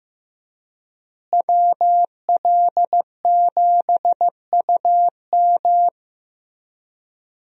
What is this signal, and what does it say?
Morse code, heard as a keyed tone, about 16 wpm, 700 Hz. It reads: WL7UM
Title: USA Call Signs - 5 Characters - Repeated 3x - 15wpm